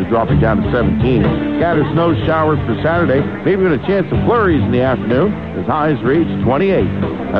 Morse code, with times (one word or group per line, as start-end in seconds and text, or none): none